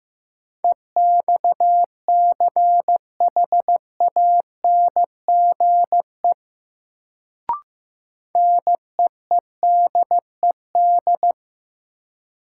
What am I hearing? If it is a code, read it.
Morse code, heard as a keyed tone, about 15 wpm, 700 Hz. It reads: EXCHANGE NEEDED
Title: Sentences from Top 1000 Words - Review Part 6 - New Word Drill - 15wpm